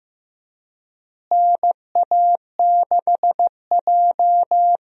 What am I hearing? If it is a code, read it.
Morse code, heard as a keyed tone, about 15 wpm, 700 Hz. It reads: NA6J
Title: USA Call Signs - 4 Characters - Repeated 3x - 15wpm